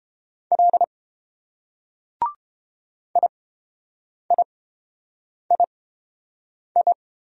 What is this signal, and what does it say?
Morse code, heard as a keyed tone, about 30 wpm, spaced as for 7 wpm, 700 Hz. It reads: L IIII